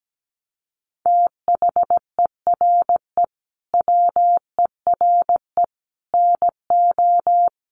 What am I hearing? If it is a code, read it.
Morse code, heard as a keyed tone, about 17 wpm, 700 Hz. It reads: THERE WERE NO
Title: Sets of 5 Words Encore2 17wpm